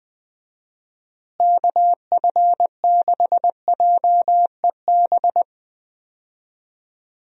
Morse code, heard as a keyed tone, about 20 wpm, 700 Hz. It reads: KF6JEB
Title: USA Call Signs - 6 Characters - Repeated 2x - 20wpm